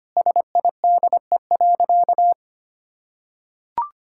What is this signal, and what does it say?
Morse code, heard as a keyed tone, about 25 wpm, 700 Hz. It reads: SIDE.
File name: practice / 25wpm / Sentences with 4-Word Phrases - 25wpm